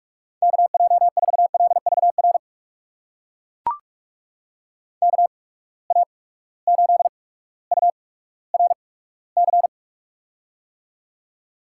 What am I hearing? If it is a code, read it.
Morse code, heard as a keyed tone, about 41 wpm, spaced as for 19 wpm, 700 Hz. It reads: KJ4LVR KA8URC
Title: USA Call Signs - 8x Character Spacing - 45wpm